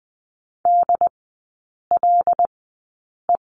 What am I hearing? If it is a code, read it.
Morse code, heard as a keyed tone, about 20 wpm, spaced as for 8 wpm, 700 Hz. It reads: DLE
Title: 114 Lesson 19e - Letter w - Three Characters 20wpm